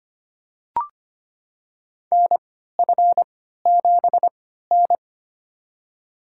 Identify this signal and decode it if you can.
Morse code, heard as a keyed tone, about 25 wpm, spaced as for 14 wpm, 700 Hz. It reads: NF7N
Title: USA Call Signs - 4 Characters - 3x Character Spacing - 25wpm